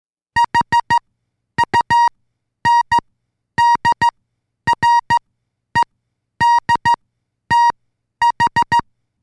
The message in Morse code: HUNDREDTH